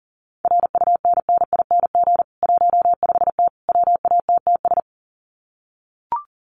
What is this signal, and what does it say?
Morse code, heard as a keyed tone, about 40 wpm, 700 Hz. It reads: RUNNING 15T WATTS
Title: Common QSO Phrases 40wpm